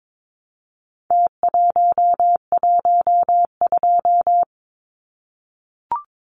Morse code, heard as a keyed tone, about 22 wpm, 700 Hz. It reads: T112
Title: Numbers - Contest Style - 4-Digit 22wpm